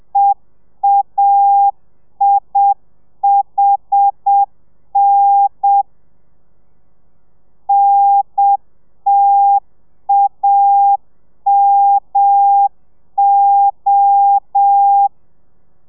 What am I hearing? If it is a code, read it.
Morse code, heard as a keyed tone, about 7 wpm, 800 Hz. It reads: EAIHN NTAMO